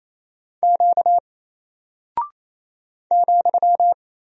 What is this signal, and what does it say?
Morse code, heard as a keyed tone, about 27 wpm, 700 Hz. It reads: Q ,
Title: Single Character (Letter:Number:Punctuation:Prosign) 28wpm